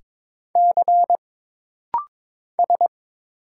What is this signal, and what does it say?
Morse code, heard as a keyed tone, about 22 wpm, 700 Hz. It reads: C S
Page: This rapid-fire timing is geared to push your ICR (Instant Character Recognition) skills.